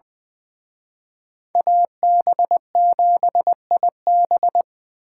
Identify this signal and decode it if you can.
Morse code, heard as a keyed tone, about 24 wpm, 700 Hz. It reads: AB7IB
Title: Speed Racing - USA 2x2 Call Signs 20wpm